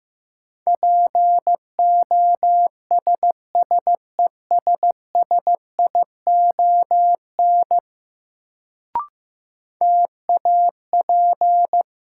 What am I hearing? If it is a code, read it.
Morse code, heard as a keyed tone, about 15 wpm, 700 Hz. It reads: POSSESSION TAP
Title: Sentences from Top 1400 Words - New Word Drill Alt - 15wpm